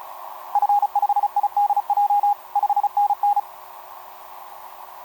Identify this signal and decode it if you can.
Morse code, heard as a keyed tone, about 31 wpm, 820 Hz. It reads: F5IDJ 5NN
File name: jt5dx f5idj etc 12m cw 18mar2011 0915